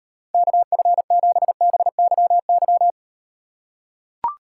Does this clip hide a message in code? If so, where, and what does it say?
Yes: Morse code from 0.3 to 2.9 s, heard as a keyed tone, about 38 wpm, 700 Hz. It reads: KF7BYY